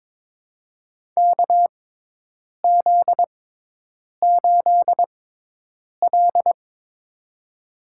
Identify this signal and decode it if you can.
Morse code, heard as a keyed tone, about 23 wpm, spaced as for 8 wpm, 700 Hz. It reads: KZ8L
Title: USA Call Signs - 4 Characters - 6x Character Spacing - 22wpm